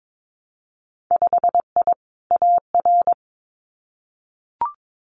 Morse code, heard as a keyed tone, about 22 wpm, 700 Hz. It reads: 5I AR
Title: Winter Field Day Exchanges Encore 22wpm